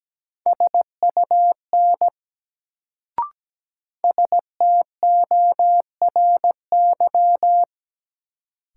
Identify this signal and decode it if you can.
Morse code, heard as a keyed tone, about 17 wpm, 700 Hz. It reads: SUN STORY
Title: Sentences from Top 300 Words - New Word Drill - 17wpm